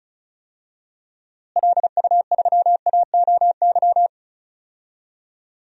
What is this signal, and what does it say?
Morse code, heard as a keyed tone, about 36 wpm, 700 Hz. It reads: LU3AOY
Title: DXCC-100 Call Signs - 6 Characters - Repeated 3x - 35wpm